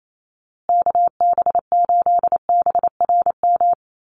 Morse code, heard as a keyed tone, about 28 wpm, 700 Hz. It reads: KB8BRM
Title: USA Call Signs Repeated 2x 28wpm